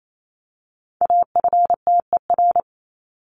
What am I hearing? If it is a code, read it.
Morse code, heard as a keyed tone, about 28 wpm, 700 Hz. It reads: AFTER
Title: Top 200 Words 28wpm